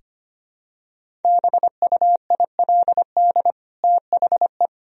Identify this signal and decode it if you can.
Morse code, heard as a keyed tone, about 25 wpm, 700 Hz. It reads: BUILD THE
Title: Sentences from Top 1000 Words - Review Part 1 - 25wpm